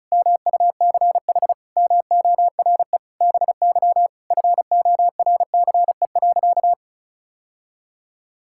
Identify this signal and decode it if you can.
Morse code, heard as a keyed tone, about 36 wpm, 700 Hz. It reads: MUCH MORE BY FORCE.
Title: Sentences from Top 1000 Words - Review Part 6 - 35wpm